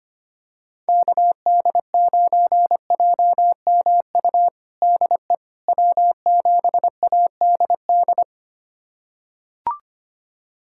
Morse code, heard as a keyed tone, about 25 wpm, 700 Hz. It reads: KD9JMU DE W7ADD
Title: USA Callsign DE Callsign 25wpm